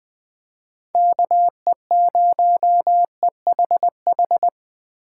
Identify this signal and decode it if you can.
Morse code, heard as a keyed tone, about 24 wpm, 700 Hz. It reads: KE0EHH
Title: Speed Racing - USA 2x3 Call Signs 20wpm